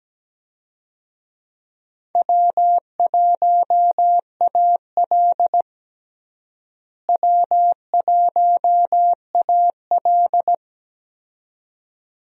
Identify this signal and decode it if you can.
Morse code, heard as a keyed tone, about 17 wpm, 700 Hz. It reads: W1AL W1AL
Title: USA Call Signs - 4 Characters - Repeated 2x - 17wpm